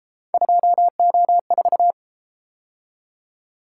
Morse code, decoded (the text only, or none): 2O4